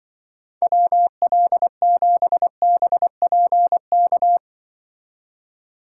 Morse code, heard as a keyed tone, about 23 wpm, 700 Hz. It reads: WL7BPK
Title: Speed Racing - USA 2x3 Call Signs 20wpm